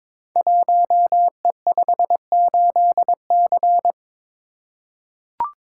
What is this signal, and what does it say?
Morse code, heard as a keyed tone, about 22 wpm, 700 Hz. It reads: 1E58C
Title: Code Groups - 5 Letter:Numbers - 22wpm